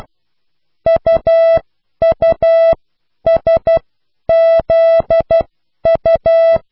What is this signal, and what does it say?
Morse code, heard as a keyed tone, about 12 wpm, spaced as for 10 wpm, 660 Hz. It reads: UUSZU